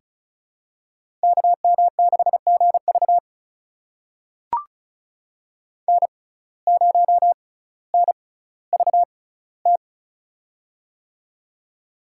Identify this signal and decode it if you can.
Morse code, heard as a keyed tone, about 34 wpm, spaced as for 18 wpm, 700 Hz. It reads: KM6GV N0NVT
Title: USA Call Signs - 5 Characters - 6x Character Spacing - 35wpm